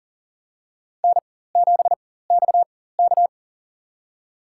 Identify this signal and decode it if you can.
Morse code, heard as a keyed tone, about 40 wpm, spaced as for 19 wpm, 700 Hz. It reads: N7XK
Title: USA Call Signs - 4x Character Spacing - 40wpm